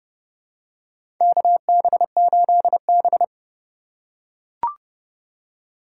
Morse code, heard as a keyed tone, about 30 wpm, 700 Hz. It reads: KB8B
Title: USA Call Signs - 4 Characters - 3x Character Spacing - 30wpm